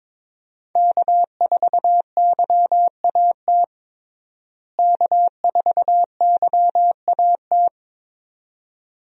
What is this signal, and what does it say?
Morse code, heard as a keyed tone, about 21 wpm, 700 Hz. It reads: K4YAT K4YAT
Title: USA Call Signs - 5 Characters - Repeated 2x - 22wpm